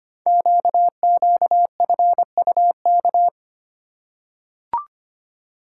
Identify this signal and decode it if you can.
Morse code, heard as a keyed tone, about 25 wpm, 700 Hz. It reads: QQFUK
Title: Code Groups - 5 Letters - 25wpm